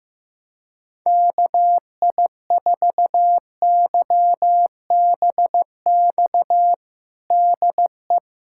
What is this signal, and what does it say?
Morse code, heard as a keyed tone, about 15 wpm, 700 Hz. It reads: KI4YBX DE
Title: USA Callsign DE Callsign - 6 Characters - 15wpm